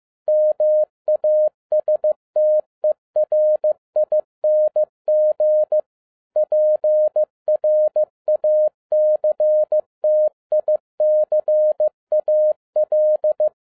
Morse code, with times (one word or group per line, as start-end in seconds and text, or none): 0.3-5.8 MASTERING
6.4-13.5 PRACTICAL